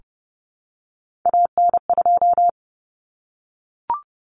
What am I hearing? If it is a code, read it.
Morse code, heard as a keyed tone, about 30 wpm, 700 Hz. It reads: AN2
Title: Common Cut Numbers - 3-Digit 30wpm